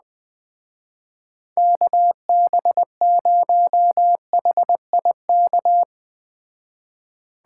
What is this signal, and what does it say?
Morse code, heard as a keyed tone, about 20 wpm, 700 Hz. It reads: KB0HIK